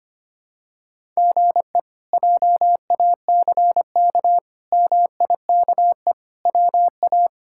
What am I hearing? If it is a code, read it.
Morse code, heard as a keyed tone, about 25 wpm, 700 Hz. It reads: GE JACK MIKE WA
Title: K1USN SST QSO Practice - 25wpm